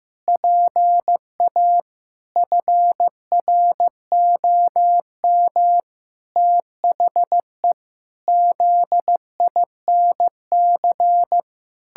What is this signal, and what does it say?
Morse code, heard as a keyed tone, about 16 wpm, 700 Hz. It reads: PA FROM THE ZINC
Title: Pangram Challenge v2 - 15wpm